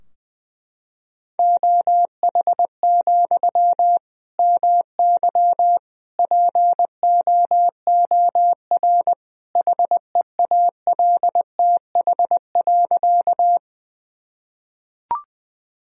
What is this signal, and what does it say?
Morse code, heard as a keyed tone, about 20 wpm, 700 Hz. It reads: OH, MY POOR HEALTH.
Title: Sentences from Top 1000 Words - Review Part 3 - 20wpm